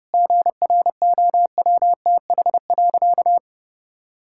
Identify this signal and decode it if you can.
Morse code, heard as a keyed tone, about 30 wpm, 700 Hz. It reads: GROWTH.
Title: Sentences from Top 1000 Words - Review Part 5 - 30wpm